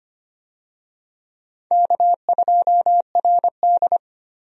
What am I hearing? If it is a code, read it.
Morse code, heard as a keyed tone, about 25 wpm, 700 Hz. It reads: K2RD
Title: USA Call Signs - 4 Characters - Repeated 2x - 25wpm